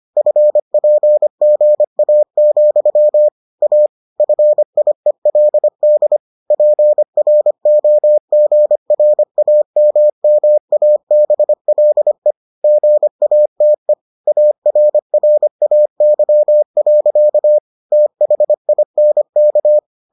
Morse code, heard as a keyed tone, about 25 wpm, 600 Hz. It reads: FPGA, A FIELD PROGRAMMABLE GATE ARRAY. THINK